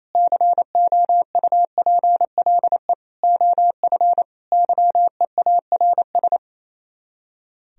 Morse code, heard as a keyed tone, about 37 wpm, 700 Hz. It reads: COUPLE OF YEARS